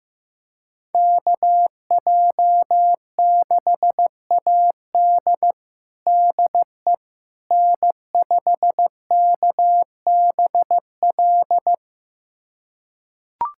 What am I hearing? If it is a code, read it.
Morse code, heard as a keyed tone, about 15 wpm, 700 Hz. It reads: KJ6AD DE N5KBL
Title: USA Callsign DE Callsign - 5 Characters - 15wpm